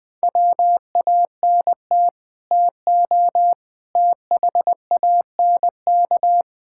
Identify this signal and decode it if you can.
Morse code, heard as a keyed tone, about 20 wpm, 700 Hz. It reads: WANT TO THANK
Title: Sets of 8 Words - 20wpm